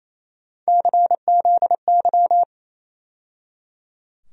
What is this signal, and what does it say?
Morse code, heard as a keyed tone, about 28 wpm, 700 Hz. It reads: CZY
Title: USA Call Sign Postfixes - 3 Letters 28wpm